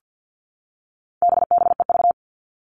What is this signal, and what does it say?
Morse code, heard as a keyed tone, about 50 wpm, 700 Hz. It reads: BBEV
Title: All Cut Numbers - 4-Digit 50wpm